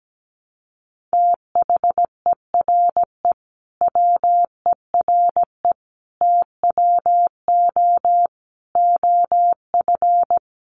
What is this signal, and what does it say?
Morse code, heard as a keyed tone, about 17 wpm, 700 Hz. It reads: THERE WERE TWO OF
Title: Sentences from Top 100 Words 17wpm